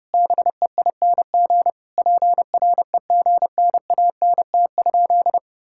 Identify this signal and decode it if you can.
Morse code, heard as a keyed tone, about 30 wpm, 700 Hz. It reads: BEING PREGNANT?
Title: Sentences from Top 1000 Words - Review Part 6 - 30wpm